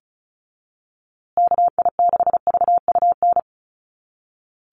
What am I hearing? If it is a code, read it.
Morse code, heard as a keyed tone, about 34 wpm, 700 Hz. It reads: KI6VUN
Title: USA Call Signs Repeated 2x 35wpm